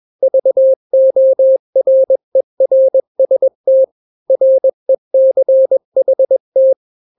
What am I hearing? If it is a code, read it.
Morse code, heard as a keyed tone, about 21 wpm, 530 Hz. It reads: VORERST RECHT
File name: daily / 20wpm